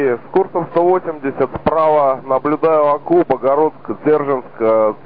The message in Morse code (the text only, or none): none